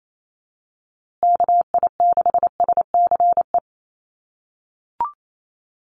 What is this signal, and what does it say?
Morse code, heard as a keyed tone, about 28 wpm, 700 Hz. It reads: KI6SCE
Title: USA Call Signs Repeated 2x 28wpm